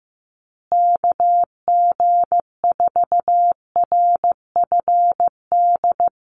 Morse code, heard as a keyed tone, about 15 wpm, 700 Hz. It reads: KG4RFD
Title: USA Call Signs Repeated 3x 15wpm